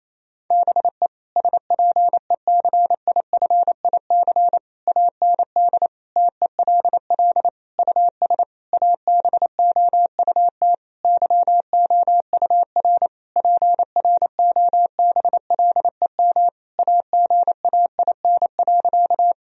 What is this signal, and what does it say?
Morse code, heard as a keyed tone, about 28 wpm, 700 Hz. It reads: BE SPECIFIC AND TELL US ABOUT YOUR PROBLEM AGAIN.
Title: Sentences from Top 1000 Words - Review Part 3 - 28wpm